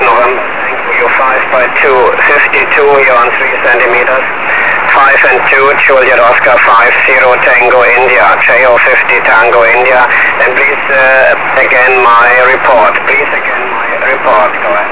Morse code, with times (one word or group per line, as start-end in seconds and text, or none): none